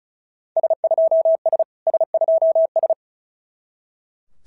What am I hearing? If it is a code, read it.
Morse code, heard as a keyed tone, about 36 wpm, 650 Hz. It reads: S2S S2S
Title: SOTA QSO Practice 35wpm